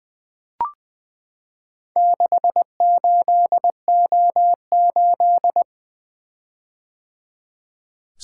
Morse code, heard as a keyed tone, about 20 wpm, 700 Hz. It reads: 68O8
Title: Four Letter-Number Combinations - 20wpm